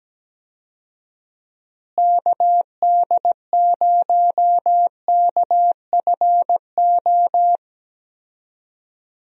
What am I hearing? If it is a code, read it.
Morse code, heard as a keyed tone, about 17 wpm, 700 Hz. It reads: KD0KFO
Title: USA Call Signs - 6 Characters - Repeated 3x - 17wpm